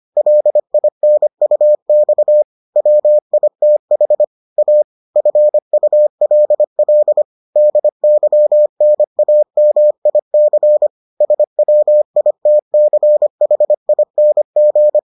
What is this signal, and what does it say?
Morse code, heard as a keyed tone, about 25 wpm, 600 Hz. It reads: LINUX WITH A FULL DYNAMIC SWITCHING